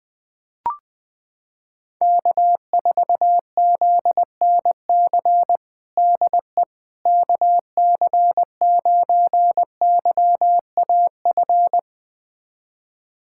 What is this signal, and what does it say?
Morse code, heard as a keyed tone, about 21 wpm, 700 Hz. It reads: K4ZNC DE KC9YAF
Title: USA Callsign DE Callsign 20wpm